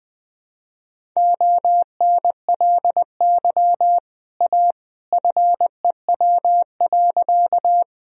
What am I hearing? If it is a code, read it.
Morse code, heard as a keyed tone, about 20 wpm, 700 Hz. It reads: ONLY A FEW.
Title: Sentences from Top 1000 Words - Review Part 1 - 20wpm